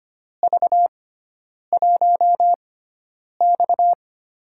Morse code, heard as a keyed tone, about 25 wpm, spaced as for 9 wpm, 700 Hz. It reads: V1X
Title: USA Call Signs - 4 Characters - 6x Character Spacing - 25wpm